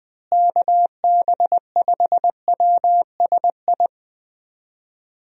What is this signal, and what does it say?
Morse code, heard as a keyed tone, about 20 wpm, 700 Hz. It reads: KB5WSI
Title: USA Call Signs - 6 Characters - Repeated 3x - 20wpm